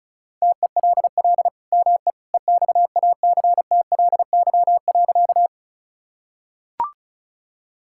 Morse code, heard as a keyed tone, about 36 wpm, 690 Hz. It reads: TELL ME EXACTLY.
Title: Sentences from Top 1000 Words - Review Part 6 - 35wpm